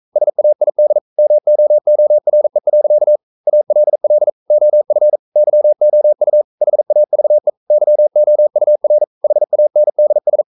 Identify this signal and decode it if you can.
Morse code, heard as a keyed tone, about 42 wpm, 600 Hz. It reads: SAID MOORE. ALL OF YOU HAVE YOUR HANDS
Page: Romanen Never af Ken Follet på engelsk.